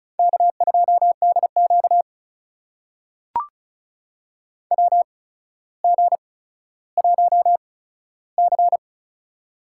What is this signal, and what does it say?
Morse code, heard as a keyed tone, about 34 wpm, spaced as for 15 wpm, 700 Hz. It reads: K2DQ WG1C
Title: USA Call Signs - 8x Character Spacing - 35wpm